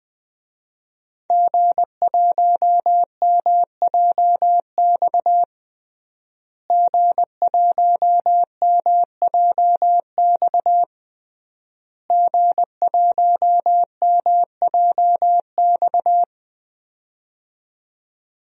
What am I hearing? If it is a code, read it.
Morse code, heard as a keyed tone, about 20 wpm, 700 Hz. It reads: G1MJX G1MJX G1MJX
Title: DXCC-100 Call Signs - 5 Characters - Repeated 3x - 20wpm